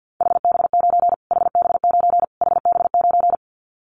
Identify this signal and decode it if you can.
Morse code, heard as a keyed tone, about 50 wpm, 700 Hz. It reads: HB9 HB9 HB9
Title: Callsign Prefixes for Top 25 DXCC - Repeated 3x - 50wpm